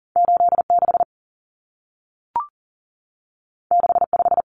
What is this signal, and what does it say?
Morse code, heard as a keyed tone, about 40 wpm, 700 Hz. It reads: 86 65
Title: Numbers - 2-Digit 40wpm